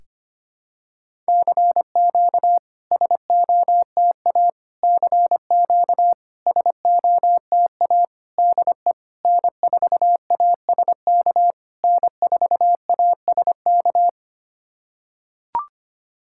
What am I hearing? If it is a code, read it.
Morse code, heard as a keyed tone, about 25 wpm, 690 Hz. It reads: CQ SOTA CQ SOTA DE N4ASK N4ASK
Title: SOTA QSO Practice 25wpm